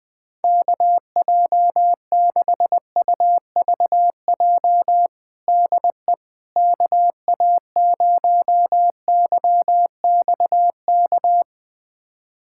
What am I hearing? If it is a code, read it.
Morse code, heard as a keyed tone, about 20 wpm, 700 Hz. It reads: KJ6UVJ DE KA0YXK
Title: USA Callsign DE Callsign - 6 Characters - 20wpm